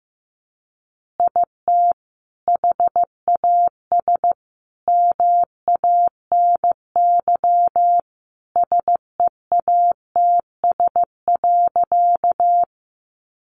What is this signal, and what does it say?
Morse code, heard as a keyed tone, about 15 wpm, 700 Hz. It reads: IT HAS MANY SEATS.
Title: Additional Easy Phrases 15wpm